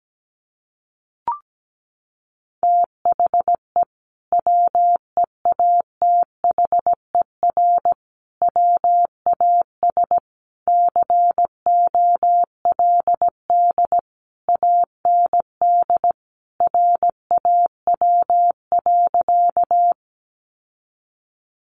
Morse code, heard as a keyed tone, about 17 wpm, 700 Hz. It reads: THE WEATHER WAS COLD AND RAW.
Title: Sentences from Top 1500 Words 17wpm